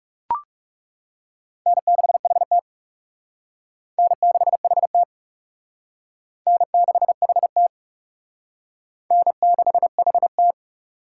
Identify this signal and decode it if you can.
Morse code, heard as a keyed tone, about 36 wpm, 700 Hz. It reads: N6HT N6HT N6HT N6HT
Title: Speed Racing - USA 1x2 Call Signs 30wpm